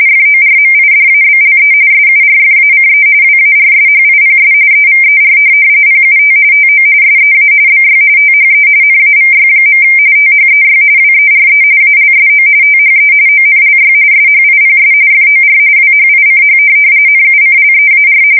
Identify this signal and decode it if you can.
RTTY, about 45.45 baud. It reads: FORMIDABLE SURFACE WARSHIPS EVER PUT TO SEA BY THE NAVY. THEY WERE ALSO THE ONLY SHIPS FOR WHICH NUCLEAR
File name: fsk170 / USS-New-Jersey